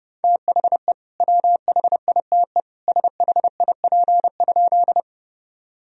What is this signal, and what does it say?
Morse code, heard as a keyed tone, about 32 wpm, 700 Hz. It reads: THE WHITE SHIP?
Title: Sentences from Top 1000 Words - Review Part 2 - 30wpm